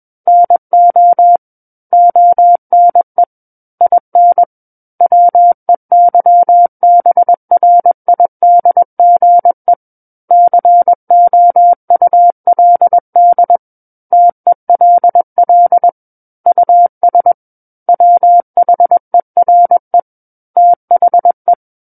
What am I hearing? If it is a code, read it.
Morse code, heard as a keyed tone, about 21 wpm, 700 Hz. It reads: NO ONE IN WEYBRIDGE COULD TELL US WHERE THE
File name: Book One / 12-Chapter 12 - 21 WPM